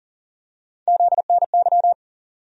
Morse code, heard as a keyed tone, about 40 wpm, 700 Hz. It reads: ZNY
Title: USA Call Sign Postfixes - 3 Letters 40wpm